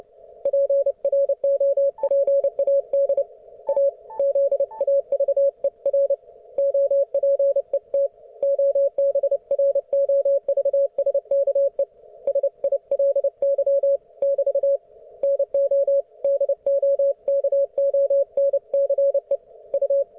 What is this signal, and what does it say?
Morse code, heard as a keyed tone, about 29 wpm, 560 Hz. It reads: PROPAD A ZAVER OPET OBROVSKE SILY <BT> NO DOKONCE U